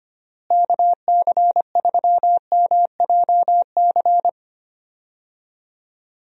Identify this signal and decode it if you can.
Morse code, heard as a keyed tone, about 25 wpm, 700 Hz. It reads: KC3MJC
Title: USA Call Signs - 6 Characters - Repeated 3x - 25wpm